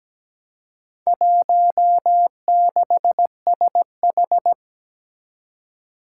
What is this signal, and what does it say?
Morse code, heard as a keyed tone, about 17 wpm, 700 Hz. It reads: 16SH
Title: Four Challenging Letter-Number Combinations 17wpm